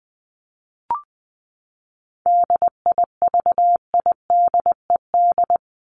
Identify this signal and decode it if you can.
Morse code, heard as a keyed tone, about 20 wpm, 700 Hz. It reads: DIVIDED
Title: ED Suffix - Top 100 Words with Base Words 20wpm